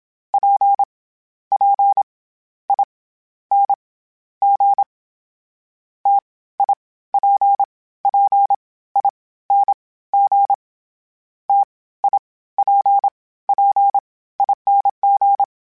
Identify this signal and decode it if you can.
Morse code, heard as a keyed tone, about 27 wpm, 800 Hz. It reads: P P I N G T I P P I N G T I P P ING
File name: Variable Speed / Variable Speed-mp3s / T AOEI's sp @25